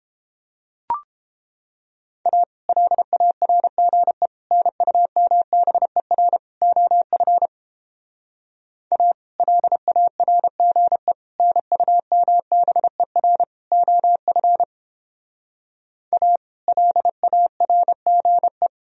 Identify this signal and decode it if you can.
Morse code, heard as a keyed tone, about 29 wpm, 700 Hz. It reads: A LARGE NUMBER OF A LARGE NUMBER OF A LARGE